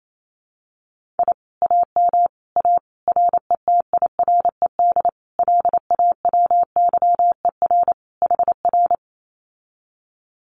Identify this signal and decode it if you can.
Morse code, heard as a keyed tone, about 28 wpm, 700 Hz. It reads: I AM A RETIRED LAWYER HR
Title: Common QSO Phrases Encore 28wpm